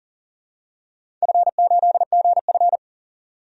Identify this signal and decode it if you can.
Morse code, heard as a keyed tone, about 41 wpm, 700 Hz. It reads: F8GF
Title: DXCC-100 Call Signs - 4 Characters - 40wpm